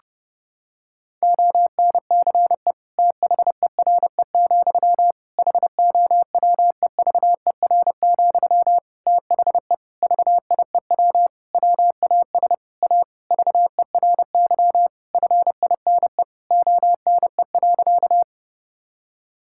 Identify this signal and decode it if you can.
Morse code, heard as a keyed tone, about 30 wpm, 700 Hz. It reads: ONCE THERE, HOWEVER, THE VIEW WAS A VERY FINE ONE.
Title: Sentences from Top 1000 Words - Review Part 6 - 30wpm